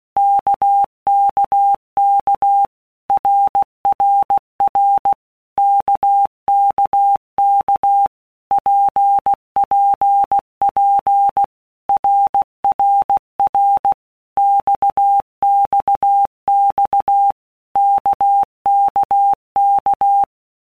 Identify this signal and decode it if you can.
Morse code, heard as a keyed tone, about 16 wpm, 800 Hz. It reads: KKK RRR KKK PPP RRR XXX KKK
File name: Basic Letter Groups By Blocks / Groups@15 / RKPX@15